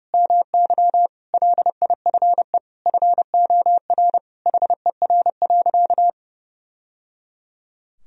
Morse code, heard as a keyed tone, about 30 wpm, 700 Hz. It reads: MY LIFE FOR HER.
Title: Sentences from Top 1000 Words - Review Part 2 - 30wpm